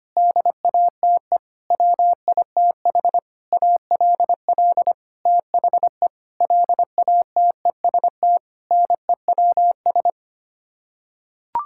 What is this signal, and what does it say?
Morse code, heard as a keyed tone, about 25 wpm, 700 Hz. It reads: DATE WITH ALL THE LATEST NEWS
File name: Sets of 9 Words - 25wpm